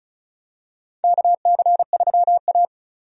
Morse code, heard as a keyed tone, about 34 wpm, 700 Hz. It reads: KC3A
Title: USA Call Signs - 4 Characters - 35wpm